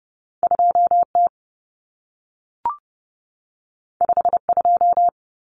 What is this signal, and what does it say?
Morse code, heard as a keyed tone, about 30 wpm, 700 Hz. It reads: 2T 52
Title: Common Cut Numbers - 2-Digit 30wpm